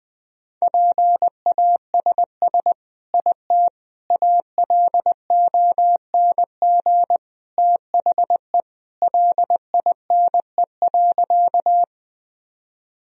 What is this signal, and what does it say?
Morse code, heard as a keyed tone, about 20 wpm, 700 Hz. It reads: PASS IT ALONG THE LINE.
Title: Sentences from Top 1000 Words - Review Part 1 - 20wpm